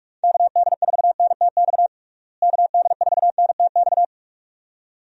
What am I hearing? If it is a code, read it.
Morse code, heard as a keyed tone, about 46 wpm, 700 Hz. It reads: KD4NTX KD4NTX
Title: USA Call Signs - 6 Characters - Repeated 2x - 45wpm